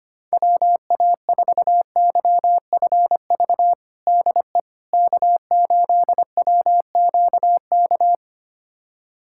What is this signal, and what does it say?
Morse code, heard as a keyed tone, about 25 wpm, 700 Hz. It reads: WA4YFV DE K8WQK
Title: USA Callsign DE Callsign 25wpm